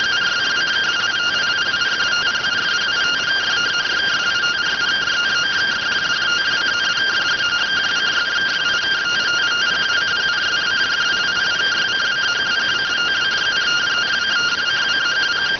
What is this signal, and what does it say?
Two-tone teleprinter signal, about 170 Hz shift.